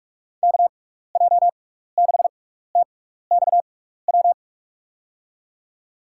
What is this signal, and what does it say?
Morse code, heard as a keyed tone, about 46 wpm, spaced as for 15 wpm, 700 Hz. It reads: KJ6TXW
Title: USA Call Signs - 6 Characters - 6x Character Spacing - 45wpm